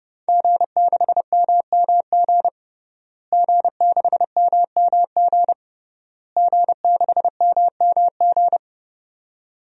Morse code, heard as a keyed tone, about 32 wpm, 700 Hz. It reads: G6MMG G6MMG G6MMG
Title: DXCC-100 Call Signs - 5 Characters - Repeated 3x - 30wpm